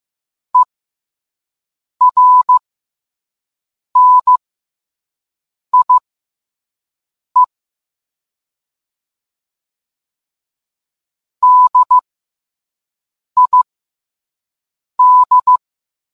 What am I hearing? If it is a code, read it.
Morse code, heard as a keyed tone, about 15 wpm, spaced as for 5 wpm, 1000 Hz. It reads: ERNIE DID